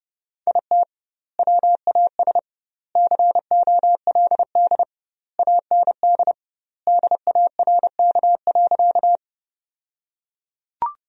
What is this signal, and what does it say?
Morse code, heard as a keyed tone, about 32 wpm, 700 Hz. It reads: IT WAS COLD AND DARK.
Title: 2x Word-Spacing - Sentences from Top 500 Words - 30wpm